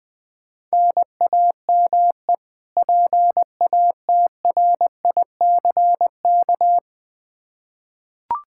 Morse code, heard as a keyed tone, about 20 wpm, 700 Hz. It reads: NAME PATRICK
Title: K1USN SST QSO Practice 20wpm